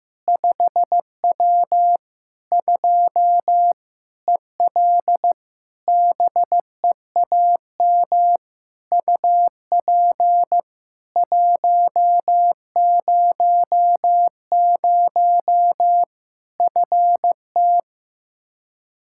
Morse code, heard as a keyed tone, about 15 wpm, 700 Hz. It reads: 5W 2 EL BEAM UP 100 FT
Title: K1USN SST QSO Practice 15wpm